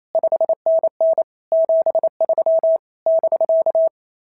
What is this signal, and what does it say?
Morse code, heard as a keyed tone, about 25 wpm, 650 Hz. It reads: 5NN 73 <BK>
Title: SOTA QSO Practice 28wpm